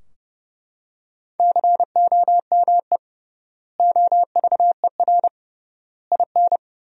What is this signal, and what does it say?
Morse code, heard as a keyed tone, about 30 wpm, 700 Hz. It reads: COME OVER IN
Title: 3x Word-Spacing - Sentences from Top 300 Words - 30wpm